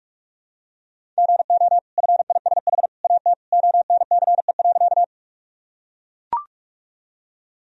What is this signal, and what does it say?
Morse code, heard as a keyed tone, about 44 wpm, 700 Hz. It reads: GO FISH AT ONCE.
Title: Sentences from Top 1000 Words - Review Part 1 - 45wpm